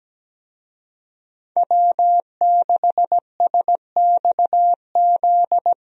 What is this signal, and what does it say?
Morse code, heard as a keyed tone, about 17 wpm, 700 Hz. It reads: W6SXZ
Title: USA Call Signs - 5 Characters - Repeated 3x - 17wpm